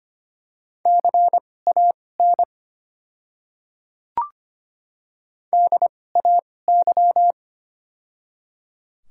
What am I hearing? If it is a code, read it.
Morse code, heard as a keyed tone, about 25 wpm, spaced as for 18 wpm, 690 Hz. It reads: CAN DAY
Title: 2x Character Spacing - Top 200 Words - 25wpm